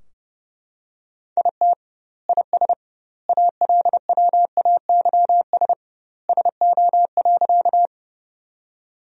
Morse code, heard as a keyed tone, about 32 wpm, 700 Hz. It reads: IT IS ALWAYS SO.
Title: 2x Word-Spacing - Sentences from Top 300 Words - 30wpm